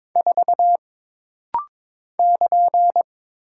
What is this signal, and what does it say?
Morse code, heard as a keyed tone, about 22 wpm, 700 Hz. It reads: 4 <KN>
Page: This rapid-fire timing is geared to push your ICR (Instant Character Recognition) skills.